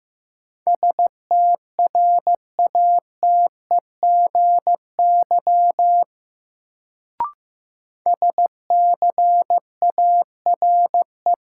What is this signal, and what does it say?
Morse code, heard as a keyed tone, about 15 wpm, 700 Hz. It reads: STRATEGY SCARE
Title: Sentences from Top 1000 Words - Review Part 5 - New Word Drill - 15wpm